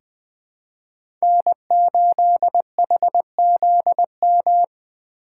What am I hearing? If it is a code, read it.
Morse code, heard as a keyed tone, about 20 wpm, 700 Hz. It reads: N8HZM
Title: K1USN SST QSO Practice - 20wpm